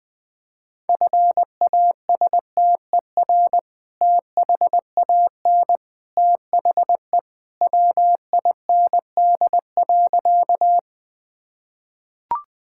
Morse code, heard as a keyed tone, about 21 wpm, 700 Hz. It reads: FASTER THAN THE WIND.
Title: Sentences with 2-Word Phrases - 20wpm